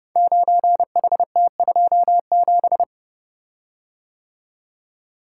Morse code, heard as a keyed tone, about 30 wpm, 700 Hz. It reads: 9HT27
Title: Code Groups - 5 Letter:Numbers - 30wpm